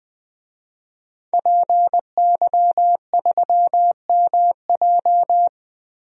Morse code, heard as a keyed tone, about 20 wpm, 700 Hz. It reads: PY3MJ
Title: DXCC-100 Call Signs - 5 Characters - Repeated 2x - 20wpm